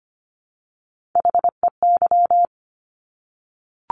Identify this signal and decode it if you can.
Morse code, heard as a keyed tone, about 25 wpm, 700 Hz. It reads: HEY
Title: Top Three Letter Combinations in Words 25wpm